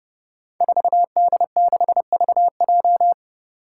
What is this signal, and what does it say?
Morse code, heard as a keyed tone, about 30 wpm, 700 Hz. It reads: 4D6VJ
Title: Five Challenging Letter-Number Combinations 30wpm